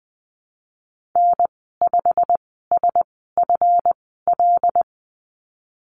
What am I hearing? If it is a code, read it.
Morse code, heard as a keyed tone, about 19 wpm, spaced as for 14 wpm, 700 Hz. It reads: N5SFL
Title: 186 Lesson 30g - Letter b - Call Signs 20wpm